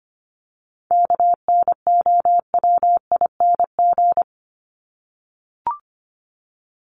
Morse code, heard as a keyed tone, about 25 wpm, 700 Hz. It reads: KNOWING
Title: ING Suffix - Top 100 Words with Base Words 25wpm